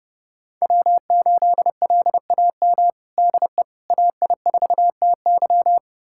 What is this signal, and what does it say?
Morse code, heard as a keyed tone, about 32 wpm, 700 Hz. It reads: W8LAM DE AI4TY
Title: USA Callsign DE Callsign - 5 Characters - 30wpm